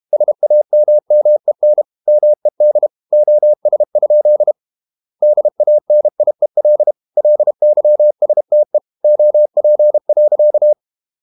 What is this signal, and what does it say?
Morse code, heard as a keyed tone, about 32 wpm, 600 Hz. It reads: SAMMEN MED OS? DANIEL LYSTE OP.
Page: Ferie 32 wpm